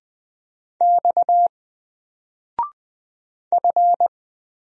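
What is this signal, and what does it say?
Morse code, heard as a keyed tone, about 20 wpm, 700 Hz. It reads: X F
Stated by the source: Single letter and number Morse drill at 20 wpm, from N4LCD.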